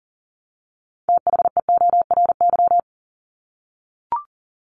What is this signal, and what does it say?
Morse code, heard as a keyed tone, about 40 wpm, 700 Hz. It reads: THEORY
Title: Top 501-750 Words 40wpm